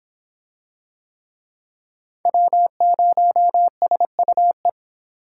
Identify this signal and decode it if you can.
Morse code, heard as a keyed tone, about 26 wpm, 700 Hz. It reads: W0SUE